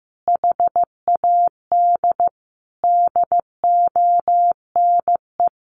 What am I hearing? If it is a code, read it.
Morse code, heard as a keyed tone, about 15 wpm, 700 Hz. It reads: HAD DONE
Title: Sentences from Top 1100 Words 15wpm